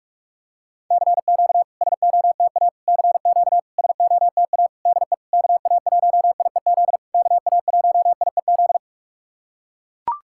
Morse code, heard as a keyed tone, about 44 wpm, 700 Hz. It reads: CQ SOTA CQ SOTA DE KA1IEZ KA1IEZ
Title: SOTA QSO Practice 45wpm